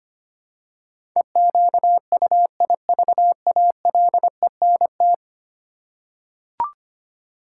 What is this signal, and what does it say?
Morse code, heard as a keyed tone, about 25 wpm, 700 Hz. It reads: EQUIVALENT
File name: 10 Letter Words - First 3 Letters Given - 25wpm